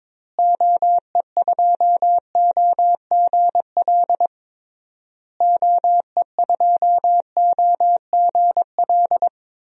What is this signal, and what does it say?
Morse code, heard as a keyed tone, about 22 wpm, 700 Hz. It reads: OE2OGL OE2OGL
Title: DXCC-100 Call Signs - 6 Characters - Repeated 3x - 22wpm